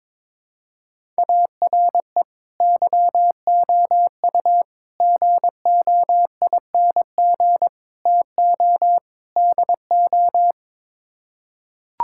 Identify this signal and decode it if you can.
Morse code, heard as a keyed tone, about 22 wpm, 700 Hz. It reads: ARE YOU GOING TO DO
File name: practice / 22wpm / Build A Phrase - 5-Word Phrases - 22wpm